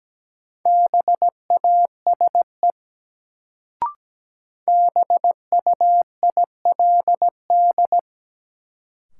Morse code, heard as a keyed tone, about 17 wpm, 700 Hz. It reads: BASE BUILD
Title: Sentences from Top 1000 Words - Review Part 1 - New Word Drill - 17wpm